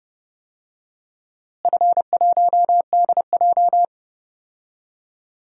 Morse code, heard as a keyed tone, about 30 wpm, 700 Hz. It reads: F1DJ
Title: DXCC-100 Call Signs - 4 Characters - Repeated 3x - 30wpm